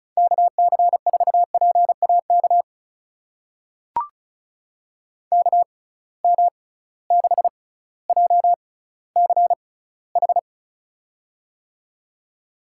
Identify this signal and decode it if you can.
Morse code, heard as a keyed tone, about 36 wpm, spaced as for 18 wpm, 700 Hz. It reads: KC4PAK KM6JCH
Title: USA Call Signs - 6 Characters - 6x Character Spacing - 35wpm